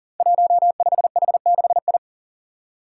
Morse code, heard as a keyed tone, about 40 wpm, 700 Hz. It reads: 15H6I
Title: Five Challenging Letter-Number Combinations 40wpm